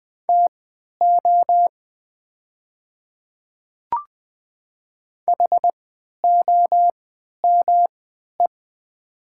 Morse code, heard as a keyed tone, about 20 wpm, spaced as for 11 wpm, 700 Hz. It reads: TO HOME
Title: 3x Character Spacing - Top 500 Words - 20wpm